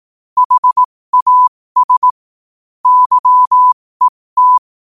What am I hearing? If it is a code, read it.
Morse code, heard as a keyed tone, about 18 wpm, spaced as for 15 wpm, 1000 Hz. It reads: HAS YET
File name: qotdfiles / 15wpm